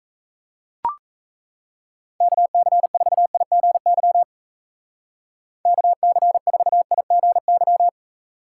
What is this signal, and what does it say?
Morse code, heard as a keyed tone, about 33 wpm, 700 Hz. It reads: KC4IGY KC4IGY
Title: Speed Racing - USA Call Signs 28wpm